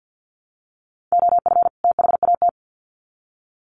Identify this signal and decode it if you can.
Morse code, heard as a keyed tone, about 50 wpm, 700 Hz. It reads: OF THAT
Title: Sets of 2 Words 50wpm